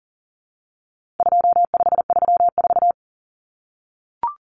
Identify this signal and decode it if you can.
Morse code, heard as a keyed tone, about 40 wpm, 700 Hz. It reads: 2534
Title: Numbers - 4-Digit 40wpm